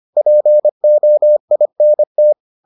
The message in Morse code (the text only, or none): POINT